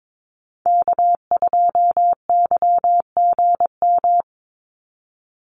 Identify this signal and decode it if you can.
Morse code, heard as a keyed tone, about 22 wpm, 700 Hz. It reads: K2YGM
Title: Call Signs - 22wpm